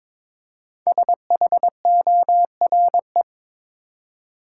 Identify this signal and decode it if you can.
Morse code, heard as a keyed tone, about 22 wpm, 700 Hz. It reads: SHORE
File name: 5 Letter Words - First 3 Letters Given - 22wpm